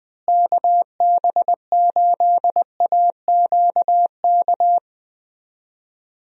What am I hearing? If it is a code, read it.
Morse code, heard as a keyed tone, about 21 wpm, 700 Hz. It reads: KB8AQK
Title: USA Call Signs - 6 Characters - Repeated 3x - 20wpm